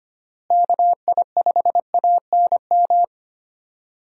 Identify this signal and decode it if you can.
Morse code, heard as a keyed tone, about 25 wpm, 700 Hz. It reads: KI5ANM
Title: USA Call Signs - 6 Characters - Repeated 3x - 25wpm